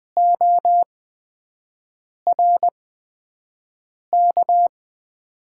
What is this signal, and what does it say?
Morse code, heard as a keyed tone, about 20 wpm, spaced as for 5 wpm, 700 Hz. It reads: ORK
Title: 8x Character Spacing - Top 200 Words - 20wpm